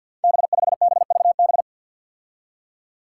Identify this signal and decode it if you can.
Morse code, heard as a keyed tone, about 53 wpm, 700 Hz. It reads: B5BVB
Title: Five Challenging Letter-Number Combinations 50wpm